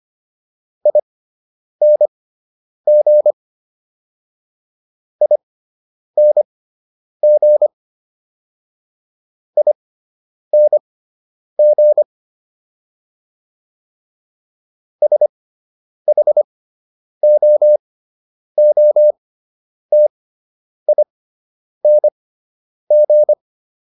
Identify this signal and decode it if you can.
Morse code, heard as a keyed tone, about 25 wpm, spaced as for 9 wpm, 600 Hz. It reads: ING ING ING SHOOTING
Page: fundamental-practice-files